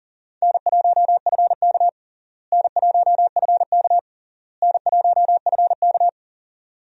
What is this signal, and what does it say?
Morse code, heard as a keyed tone, about 40 wpm, 700 Hz. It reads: N1FK N1FK N1FK
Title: USA Call Signs - 4 Characters - Repeated 3x - 40wpm